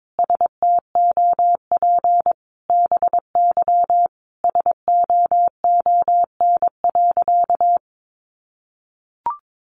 Morse code, heard as a keyed tone, about 22 wpm, 700 Hz. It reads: STOP BY SOON.
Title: Sentences from Top 1500 Words 22wpm